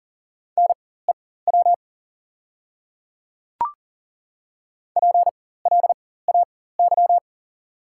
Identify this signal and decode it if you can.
Morse code, heard as a keyed tone, about 40 wpm, spaced as for 19 wpm, 700 Hz. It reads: NEW PLAY
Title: 4x Character Spacing - Top 200 Words - 40wpm